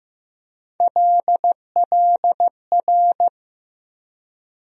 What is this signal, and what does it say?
Morse code, heard as a keyed tone, about 15 wpm, 700 Hz. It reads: LLR
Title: USA Call Sign Postfixes - 3 Letters 15wpm